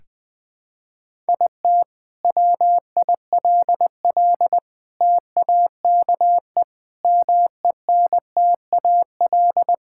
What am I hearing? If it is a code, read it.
Morse code, heard as a keyed tone, about 20 wpm, 700 Hz. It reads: IT WILL TAKE MENTAL
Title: Sentences from Top 1000 Words - Review Part 6 - 20wpm